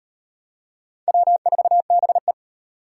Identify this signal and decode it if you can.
Morse code, heard as a keyed tone, about 34 wpm, 700 Hz. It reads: W4BE